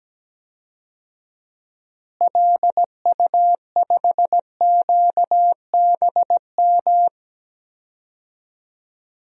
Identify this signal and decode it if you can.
Morse code, heard as a keyed tone, about 17 wpm, 700 Hz. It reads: LU5QBM
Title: DXCC-100 Call Signs - 6 Characters - Repeated 2x - 17wpm